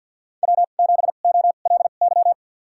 Morse code, heard as a keyed tone, about 51 wpm, spaced as for 36 wpm, 700 Hz. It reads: W7OLY
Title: USA Call Signs - 5 Characters - 2x Character Spacing - 50wpm